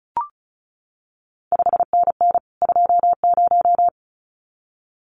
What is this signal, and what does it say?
Morse code, heard as a keyed tone, about 36 wpm, 700 Hz. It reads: 5NN 20
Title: CQ WW DX Contest Exchange Encore - 35wpm